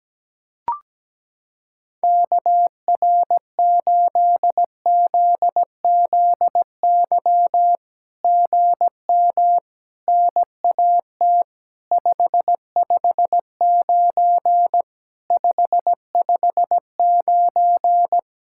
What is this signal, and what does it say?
Morse code, heard as a keyed tone, about 17 wpm, 700 Hz. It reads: KR8ZZY GM NAT 559 559
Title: SOTA QSO Practice 17wpm